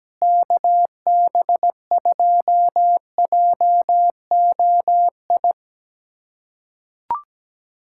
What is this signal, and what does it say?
Morse code, heard as a keyed tone, about 17 wpm, 700 Hz. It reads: KB2JOI
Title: USA Callsign DE Callsign - 6 Characters - 17wpm